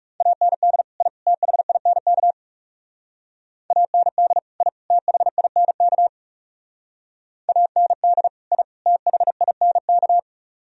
Morse code, heard as a keyed tone, about 38 wpm, 700 Hz. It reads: AND I THINK AND I THINK AND I THINK